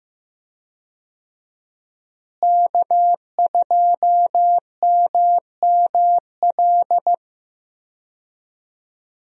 Morse code, heard as a keyed tone, about 15 wpm, 700 Hz. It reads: K2MML
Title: USA Call Signs - 5 Characters - Repeated 3x - 15wpm